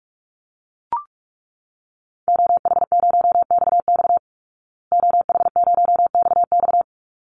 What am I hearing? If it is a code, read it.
Morse code, heard as a keyed tone, about 44 wpm, 700 Hz. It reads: OH0XX OH0XX
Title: Call Signs Repeated 3x - 45wpm